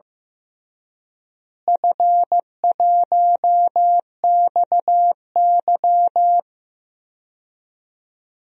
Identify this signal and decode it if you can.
Morse code, heard as a keyed tone, about 15 wpm, 700 Hz. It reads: F1XY
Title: DXCC-100 Call Signs - 4 Characters - Repeated 2x - 15wpm